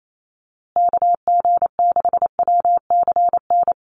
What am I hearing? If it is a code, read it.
Morse code, heard as a keyed tone, about 28 wpm, 700 Hz. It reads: KG6WCN
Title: USA Call Signs Repeated 3x 28wpm